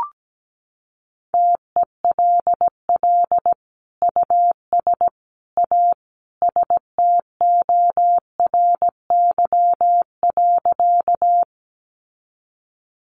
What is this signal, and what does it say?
Morse code, heard as a keyed tone, about 17 wpm, 700 Hz. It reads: TELL US A STORY.
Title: Sentences from Top 300 Words 17wpm